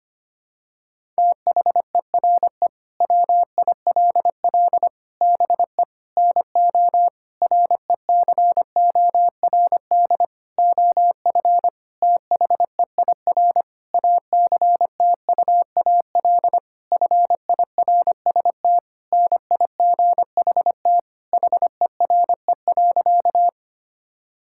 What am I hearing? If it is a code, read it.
Morse code, heard as a keyed tone, about 25 wpm, 700 Hz. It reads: THERE WILL BE NO RECORD OF THEIR ACTUAL FIRST NIGHT HERE.
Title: Sentences from Top 1000 Words - Review Part 5 - 25wpm